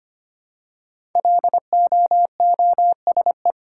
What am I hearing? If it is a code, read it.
Morse code, heard as a keyed tone, about 25 wpm, 700 Hz. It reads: LOOSE
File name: 5 Letter Words - First 3 Letters Given - 25wpm